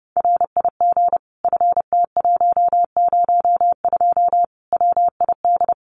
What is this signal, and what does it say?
Morse code, heard as a keyed tone, about 30 wpm, 700 Hz. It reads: RIG FT102 WID
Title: Common QSO Phrases 30wpm